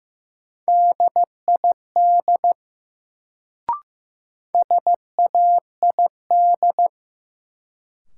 Morse code, heard as a keyed tone, about 15 wpm, 700 Hz. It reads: DID SAID
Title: Sentences from Top 100 Words - New Word Drill - 15wpm